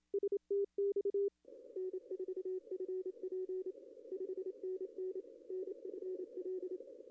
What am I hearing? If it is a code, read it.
Morse code, heard as a keyed tone, about 27 wpm, 400 Hz. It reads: STX N4FP 5NN NFL
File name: contests / arrl-160-k5tr-op-w5tn -0800